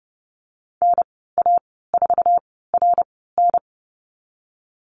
Morse code, heard as a keyed tone, about 30 wpm, spaced as for 17 wpm, 700 Hz. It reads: NA4RN
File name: Lesson 19g - Letter w - Call Signs 30wpm